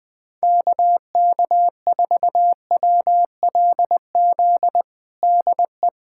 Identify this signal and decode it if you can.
Morse code, heard as a keyed tone, about 20 wpm, 700 Hz. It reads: KK4WLZ DE
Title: USA Callsign DE Callsign - 6 Characters - 20wpm